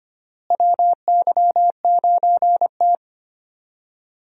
Morse code, heard as a keyed tone, about 25 wpm, 700 Hz. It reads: WY9T
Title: USA Call Signs - 4 Characters - Repeated 3x - 25wpm